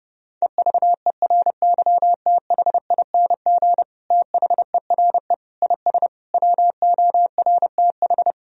Morse code, heard as a keyed tone, about 32 wpm, 700 Hz. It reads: EVERYTHING THERE IS WORTH
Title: Sentences from Top 1000 Words - Review Part 1 - 30wpm